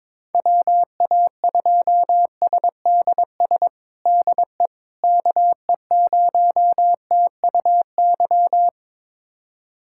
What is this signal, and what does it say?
Morse code, heard as a keyed tone, about 22 wpm, 700 Hz. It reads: WA2SDS DE KE0TUY
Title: USA Callsign DE Callsign - 6 Characters - 22wpm